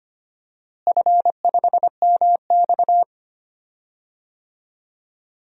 Morse code, heard as a keyed tone, about 25 wpm, 700 Hz. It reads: F5MX
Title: DXCC-100 Call Signs - 4 Characters - Repeated 3x - 25wpm